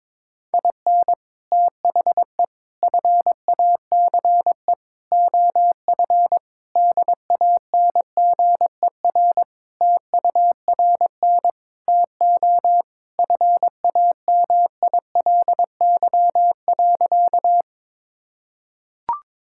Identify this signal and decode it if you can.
Morse code, heard as a keyed tone, about 22 wpm, 700 Hz. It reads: IN THE FACE OF DANGER TURN TO FAMILY.
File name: practice / 22wpm / Sentences with 3-Word Phrases - 22wpm